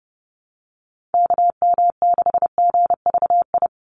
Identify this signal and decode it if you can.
Morse code, heard as a keyed tone, about 30 wpm, 700 Hz. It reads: KM6GVI
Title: USA Call Signs 30wpm